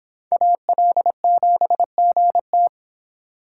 Morse code, heard as a keyed tone, about 25 wpm, 700 Hz. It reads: AL7GT